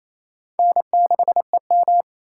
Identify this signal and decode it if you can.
Morse code, heard as a keyed tone, about 36 wpm, 700 Hz. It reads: N6EM